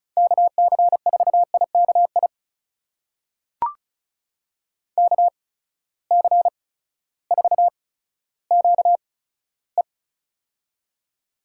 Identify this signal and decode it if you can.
Morse code, heard as a keyed tone, about 36 wpm, spaced as for 16 wpm, 700 Hz. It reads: KC4IKI KC4QE
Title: USA Call Signs - 8x Character Spacing - 35wpm